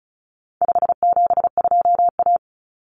Morse code, heard as a keyed tone, about 34 wpm, 700 Hz. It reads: 572A
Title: Common Cut Numbers - 4-Digit 35wpm